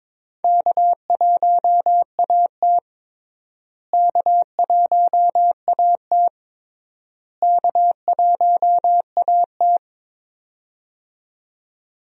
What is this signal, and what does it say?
Morse code, heard as a keyed tone, about 22 wpm, 700 Hz. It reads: K1AT K1AT K1AT
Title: USA Call Signs - 4 Characters - Repeated 3x - 22wpm